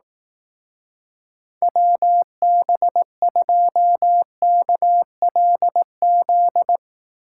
Morse code, heard as a keyed tone, about 19 wpm, 700 Hz. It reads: WB2KLZ
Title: Speed Racing - USA 2x3 Call Signs 15wpm